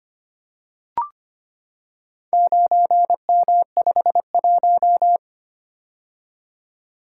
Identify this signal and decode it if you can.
Morse code, heard as a keyed tone, about 25 wpm, 700 Hz. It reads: 9M51
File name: practice / 25wpm / Four Letter-Number Combinations - 25wpm